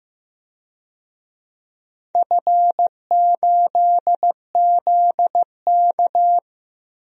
Morse code, heard as a keyed tone, about 15 wpm, 700 Hz. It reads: F8ZK
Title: DXCC-100 Call Signs - 4 Characters - 15wpm